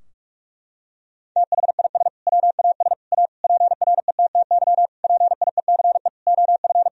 Morse code, heard as a keyed tone, about 46 wpm, 700 Hz. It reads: THIS WAS A PRETTY PIECE OF
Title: Sentences from Top 1000 Words - Review Part 2 - 45wpm